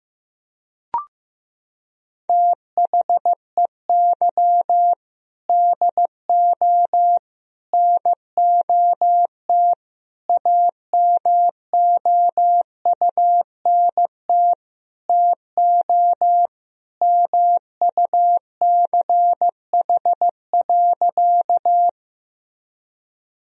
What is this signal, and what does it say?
Morse code, heard as a keyed tone, about 15 wpm, 700 Hz. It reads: THEY DO NOT AMOUNT TO MUCH.
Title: Sentences from Top 1000 Words - Review Part 4 - 15wpm